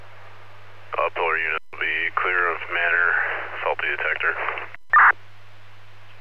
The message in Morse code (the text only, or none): none